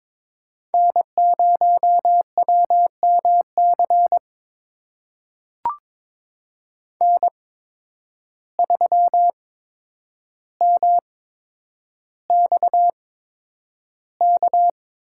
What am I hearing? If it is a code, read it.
Morse code, heard as a keyed tone, about 22 wpm, spaced as for 9 wpm, 700 Hz. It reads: N0WMC N3MXK
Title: USA Call Signs - 5 Characters - 8x Character Spacing - 22wpm